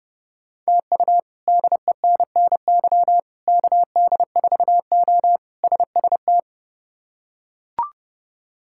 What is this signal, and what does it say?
Morse code, heard as a keyed tone, about 30 wpm, 700 Hz. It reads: TU DENNY KD4O SST
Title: K1USN SST QSO Practice - 30wpm